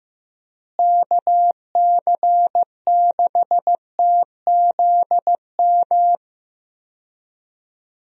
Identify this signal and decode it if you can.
Morse code, heard as a keyed tone, about 15 wpm, 700 Hz. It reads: KC6TZM
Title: USA Call Signs - 6 Characters - Repeated 2x - 15wpm